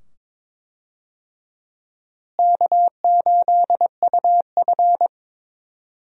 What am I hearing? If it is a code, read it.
Morse code, heard as a keyed tone, about 22 wpm, 700 Hz. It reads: K8UF
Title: USA Call Signs - 4 Characters - Repeated 2x - 22wpm